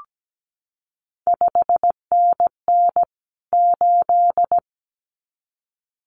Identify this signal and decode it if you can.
Morse code, heard as a keyed tone, about 17 wpm, 700 Hz. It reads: 5NN 8
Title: CQ WW DX Contest Exchange - 17wpm